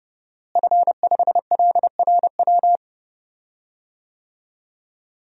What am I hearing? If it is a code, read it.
Morse code, heard as a keyed tone, about 30 wpm, 700 Hz. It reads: F5LRW
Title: DXCC-100 Call Signs - 5 Characters - Repeated 2x - 30wpm